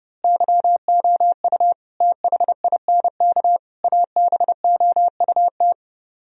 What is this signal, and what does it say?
Morse code, heard as a keyed tone, about 28 wpm, 700 Hz. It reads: YOU THINK ABOUT
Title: Speed-Racing - Sets of 5 Words 30wpm